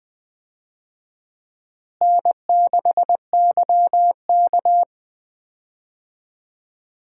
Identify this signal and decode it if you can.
Morse code, heard as a keyed tone, about 20 wpm, 700 Hz. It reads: N6YK
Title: USA Call Signs - 4 Characters - Repeated 3x - 20wpm